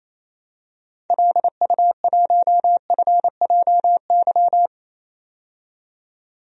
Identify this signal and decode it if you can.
Morse code, heard as a keyed tone, about 28 wpm, 700 Hz. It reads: LU1FJY
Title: DXCC-100 Call Signs - 6 Characters - 28wpm